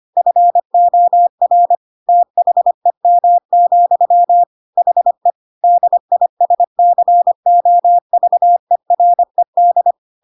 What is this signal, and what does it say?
Morse code, heard as a keyed tone, about 25 wpm, 700 Hz. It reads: FOR THEM, HE DISCOVERED
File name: Pinocchio / Pinocchio ch3 25 25 ik1hge